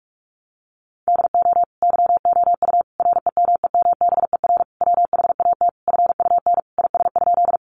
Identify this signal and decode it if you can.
Morse code, heard as a keyed tone, about 44 wpm, 700 Hz. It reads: DO YOU REMEMBER WHAT FUN IS?
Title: Sentences from Top 700 Words 45wpm